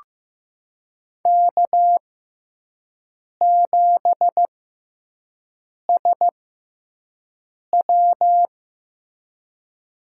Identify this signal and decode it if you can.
Morse code, heard as a keyed tone, about 15 wpm, spaced as for 5 wpm, 700 Hz. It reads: K7SW
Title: USA Call Signs - 6x Character Spacing - 15wpm